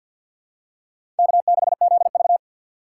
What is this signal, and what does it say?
Morse code, heard as a keyed tone, about 50 wpm, 690 Hz. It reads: K6ZV
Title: USA Call Signs - 3x Character Spacing - 50wpm